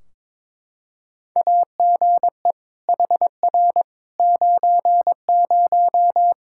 Morse code, heard as a keyed tone, about 22 wpm, 700 Hz. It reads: AGE HR 90
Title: Practice QSO - Exchange 3 - More Info - 22wpm